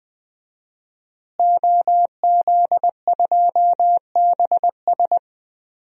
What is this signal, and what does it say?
Morse code, heard as a keyed tone, about 21 wpm, 700 Hz. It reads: OZ2BS
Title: DXCC-100 Call Signs - 5 Characters - Repeated 2x - 20wpm